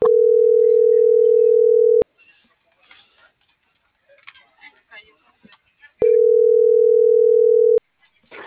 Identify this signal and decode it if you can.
Ringback tone (North America), 440 + 480 Hz.